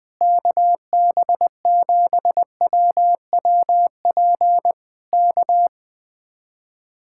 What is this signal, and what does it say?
Morse code, heard as a keyed tone, about 20 wpm, 700 Hz. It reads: KB7WWP K
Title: Speed-Racing Common QSO Phrases Encore - 17wpm